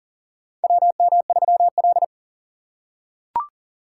Morse code, heard as a keyed tone, about 40 wpm, 700 Hz. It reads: WM3L
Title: USA Call Signs - 4 Characters - 4x Character Spacing - 40wpm